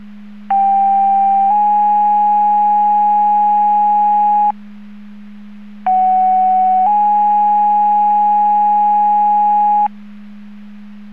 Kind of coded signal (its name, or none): two-tone page